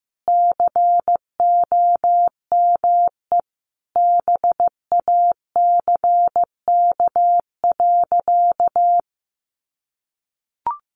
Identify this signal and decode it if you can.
Morse code, heard as a keyed tone, about 15 wpm, 700 Hz. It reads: COME BACK.
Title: Sentences from Top 1500 Words 15wpm